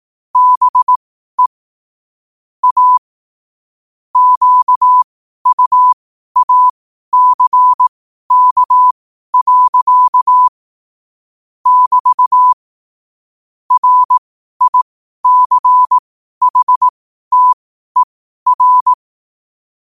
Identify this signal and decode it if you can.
Morse code, heard as a keyed tone, about 18 wpm, spaced as for 12 wpm, 1000 Hz. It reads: BE A QUACK. <BT> RICHTER
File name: qotdfiles / 12wpm